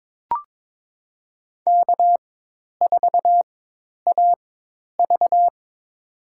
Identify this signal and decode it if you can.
Morse code, heard as a keyed tone, about 22 wpm, spaced as for 10 wpm, 700 Hz. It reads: K4AV
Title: USA Call Signs - 4 Characters - 4x Character Spacing - 22wpm